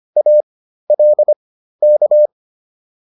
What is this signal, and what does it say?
Morse code, heard as a keyed tone, about 25 wpm, spaced as for 13 wpm, 600 Hz. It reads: ALK
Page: intermediate-practice-files